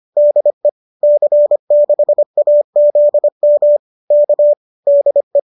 Morse code, heard as a keyed tone, about 25 wpm, 590 Hz. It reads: DE C6AZM K DE
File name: qso204 25